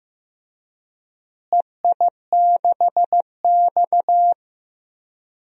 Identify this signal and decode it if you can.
Morse code, heard as a keyed tone, about 15 wpm, 700 Hz. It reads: EI6X
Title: DXCC-100 Call Signs - 4 Characters - Repeated 3x - 15wpm